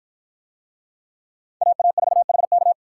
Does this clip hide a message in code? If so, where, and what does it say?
Yes: Morse code from 1.6 to 2.7 s, heard as a keyed tone, about 49 wpm, 700 Hz. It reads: AA4HK